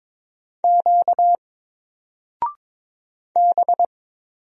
Morse code, heard as a keyed tone, about 22 wpm, 700 Hz. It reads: Q B
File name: Single-Letter-Number-22wpm